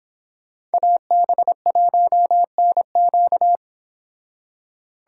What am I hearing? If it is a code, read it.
Morse code, heard as a keyed tone, about 28 wpm, 700 Hz. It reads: AB1NQ